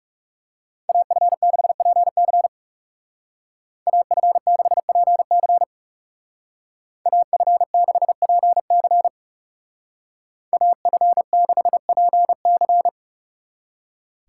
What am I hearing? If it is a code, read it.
Morse code, heard as a keyed tone, about 37 wpm, 700 Hz. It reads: AF6PC AF6PC AF6PC AF6PC
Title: Speed Racing - USA 2x2 Call Signs 30wpm